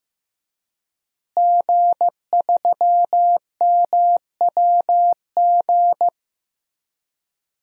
Morse code, heard as a keyed tone, about 15 wpm, 700 Hz. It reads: G3MWG
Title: DXCC-100 Call Signs - 5 Characters - Repeated 3x - 15wpm